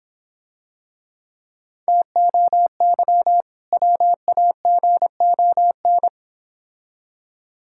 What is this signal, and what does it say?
Morse code, heard as a keyed tone, about 26 wpm, 690 Hz. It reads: TOY WAGON